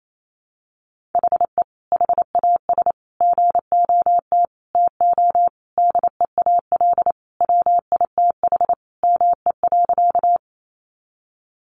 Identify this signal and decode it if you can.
Morse code, heard as a keyed tone, about 28 wpm, 700 Hz. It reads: HE HAS GOT TO DEAL WITH ME.
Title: Sentences from Top 800 Words 28wpm